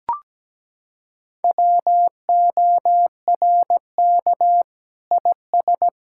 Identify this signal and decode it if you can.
Morse code, heard as a keyed tone, about 17 wpm, 700 Hz. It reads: WORK IS
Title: Sentences from Top 1000 Words - Review Part 6 - 17wpm